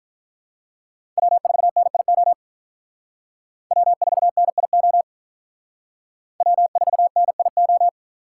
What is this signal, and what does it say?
Morse code, heard as a keyed tone, about 45 wpm, 700 Hz. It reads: W4NIO W4NIO W4NIO